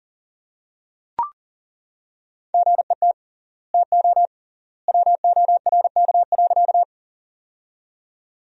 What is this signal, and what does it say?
Morse code, heard as a keyed tone, about 40 wpm, 700 Hz. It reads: GET TO WORK.
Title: 3x Word-Spacing - Sentences from Top 200 Words - 40wpm